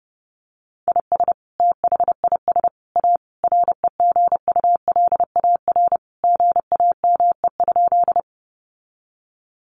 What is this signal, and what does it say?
Morse code, heard as a keyed tone, about 30 wpm, 700 Hz. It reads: IS THIS A REGULAR GAME?
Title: Sentences from Top 900 Words 30wpm